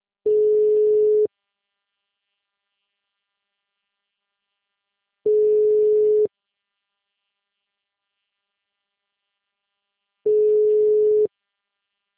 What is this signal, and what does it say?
Ringback tone (Europe, 425 Hz).